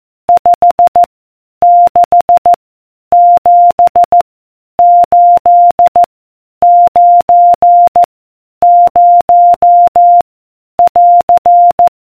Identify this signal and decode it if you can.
Morse code, heard as a keyed tone, about 14 wpm, 700 Hz. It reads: 5 6 7 8 9 0 <AR>